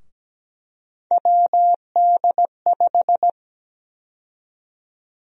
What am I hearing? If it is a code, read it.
Morse code, heard as a keyed tone, about 17 wpm, 700 Hz. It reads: WD5
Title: USA Call Sign Prefixes - Repeated 2x - 17wpm